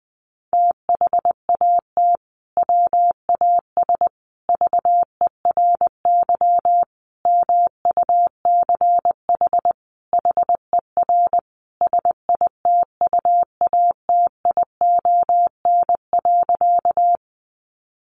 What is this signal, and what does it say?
Morse code, heard as a keyed tone, about 20 wpm, 700 Hz. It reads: THAT WAS VERY MUCH HER SITUATION.
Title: Sentences from Top 1200 Words 20wpm